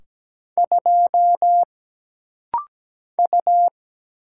Morse code, heard as a keyed tone, about 17 wpm, 700 Hz. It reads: 2 U
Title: Single Letter-Number - Rapid-Fire 17wpm